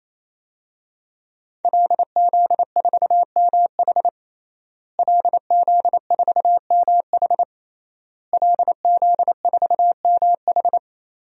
Morse code, heard as a keyed tone, about 27 wpm, 700 Hz. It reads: LZ4MH LZ4MH LZ4MH
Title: DXCC-100 Call Signs - 5 Characters - Repeated 3x - 28wpm